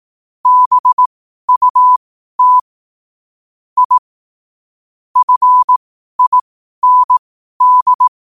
Morse code, heard as a keyed tone, about 18 wpm, spaced as for 12 wpm, 1000 Hz. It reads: BUT I FIND
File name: qotdfiles / 12wpm